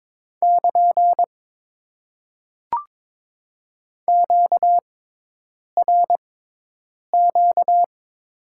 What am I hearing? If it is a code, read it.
Morse code, heard as a keyed tone, about 22 wpm, spaced as for 8 wpm, 700 Hz. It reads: <KN> QRQ
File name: Top 90 QSO Elements - 6x Character Spacing - 22wpm